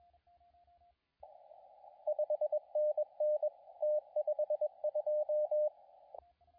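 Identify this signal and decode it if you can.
Morse code, heard as a keyed tone, about 22 wpm, 600 Hz, with radio noise. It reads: 5NN T52
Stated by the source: CW contest audio recorded at LA4C during the SAC CW contest.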